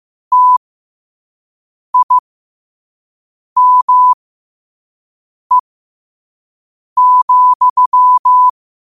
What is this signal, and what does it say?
Morse code, heard as a keyed tone, about 15 wpm, spaced as for 5 wpm, 1000 Hz. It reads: TIME,
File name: qotdfiles / 5wpm